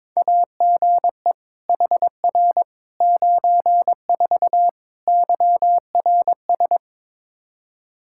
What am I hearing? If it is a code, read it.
Morse code, heard as a keyed tone, about 22 wpm, 700 Hz. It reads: AGE HR 94 YRS
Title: Practice QSO - Exchange 3 - More Info - 22wpm